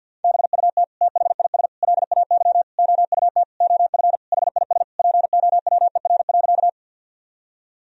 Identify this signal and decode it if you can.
Morse code, heard as a keyed tone, about 50 wpm, 700 Hz. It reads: BUT THIS LAY OUT OF HIS POWER.
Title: Sentences from Top 1000 Words - Review Part 2 - 50wpm